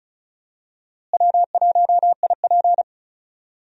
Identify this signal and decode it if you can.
Morse code, heard as a keyed tone, about 34 wpm, 700 Hz. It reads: W1IP
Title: USA Call Signs - 4 Characters - Repeated 2x - 35wpm